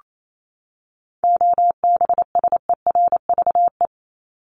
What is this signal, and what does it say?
Morse code, heard as a keyed tone, about 28 wpm, 700 Hz. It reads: OBSERVE
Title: ED Suffix - Top 100 Words with Base Words 28wpm